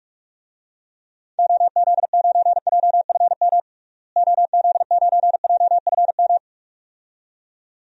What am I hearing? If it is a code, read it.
Morse code, heard as a keyed tone, about 42 wpm, 700 Hz. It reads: OZ9JFM OZ9JFM
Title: DXCC-100 Call Signs - 6 Characters - Repeated 2x - 45wpm